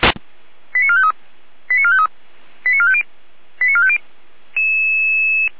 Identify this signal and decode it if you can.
Five-tone selective call (ZVEI), heard as quick runs of five tones, tones of about 80 ms. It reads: 89453 89453 89450 89450
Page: ZVEI